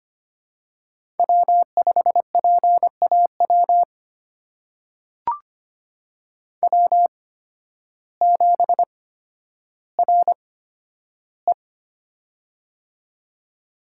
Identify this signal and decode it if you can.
Morse code, heard as a keyed tone, about 25 wpm, spaced as for 12 wpm, 700 Hz. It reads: W5PAW W7RE
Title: USA Call Signs - 8x Character Spacing - 25wpm